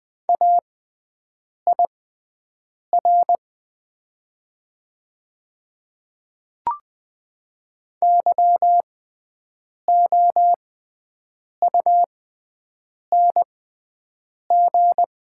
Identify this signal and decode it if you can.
Morse code, heard as a keyed tone, about 20 wpm, spaced as for 7 wpm, 700 Hz. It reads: AIR YOUNG
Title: 6x Character Spacing - Top 500 Words - 20wpm